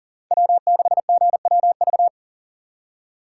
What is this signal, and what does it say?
Morse code, heard as a keyed tone, about 52 wpm, 700 Hz. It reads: W6GWV